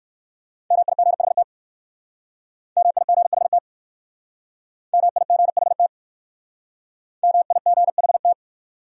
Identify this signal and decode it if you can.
Morse code, heard as a keyed tone, about 54 wpm, 700 Hz. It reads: MIGHT MIGHT MIGHT MIGHT
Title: Speed-Racing Top 300 Words 45wpm